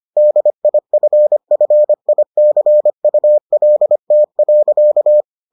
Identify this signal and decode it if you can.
Morse code, heard as a keyed tone, about 25 wpm, 600 Hz. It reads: DIFFICULT.